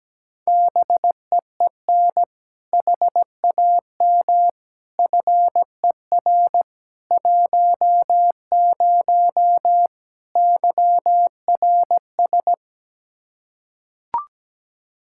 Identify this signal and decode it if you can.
Morse code, heard as a keyed tone, about 17 wpm, 700 Hz. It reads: BEEN HAM FER 10 YRS
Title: Practice QSO - Exchange 3 - More Info - 17wpm